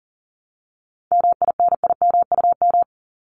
Morse code, heard as a keyed tone, about 39 wpm, 700 Hz. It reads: MINIMUM
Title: Top 1251-1500 Words 40wpm